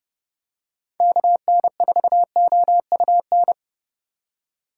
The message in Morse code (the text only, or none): KN4OUN